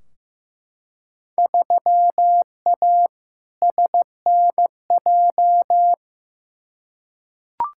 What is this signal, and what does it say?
Morse code, heard as a keyed tone, about 15 wpm, 700 Hz. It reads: 3A SNJ
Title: Field Day Exchanges - 15wpm